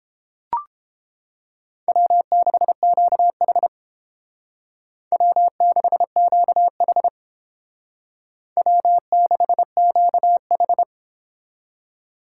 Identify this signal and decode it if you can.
Morse code, heard as a keyed tone, about 30 wpm, 700 Hz. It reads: W6QH W6QH W6QH